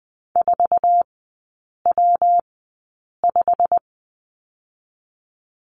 Morse code, heard as a keyed tone, about 20 wpm, 700 Hz. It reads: 4 W 5
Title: 114 Lesson 19e - Letter w - Three Characters 20wpm